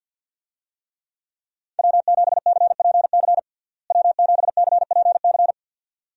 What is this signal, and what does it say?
Morse code, heard as a keyed tone, about 49 wpm, 700 Hz. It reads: W7CPC W7CPC
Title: USA Call Signs - 5 Characters - Repeated 2x - 50wpm